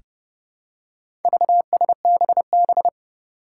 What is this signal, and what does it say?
Morse code, heard as a keyed tone, about 30 wpm, 700 Hz. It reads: VSBB
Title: Four Challenging Letter-Number Combinations 30wpm